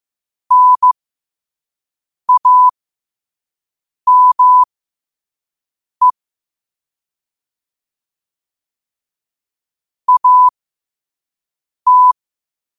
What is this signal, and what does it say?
Morse code, heard as a keyed tone, about 15 wpm, spaced as for 5 wpm, 1000 Hz. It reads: NAME AT
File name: qotdfiles / 5wpm